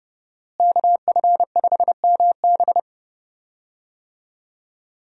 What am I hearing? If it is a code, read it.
Morse code, heard as a keyed tone, about 32 wpm, 700 Hz. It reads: KF5MB
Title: USA Call Signs - 5 Characters - Repeated 2x - 30wpm